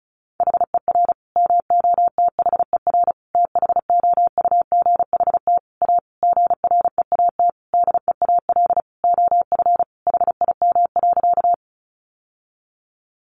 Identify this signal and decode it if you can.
Morse code, heard as a keyed tone, about 34 wpm, 700 Hz. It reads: HER MOTHER THOUGHT A GREAT DEAL OF HIM.
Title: Sentences from Top 800 Words 35wpm